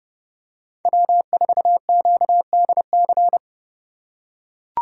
Morse code, heard as a keyed tone, about 29 wpm, 700 Hz. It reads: W4QDC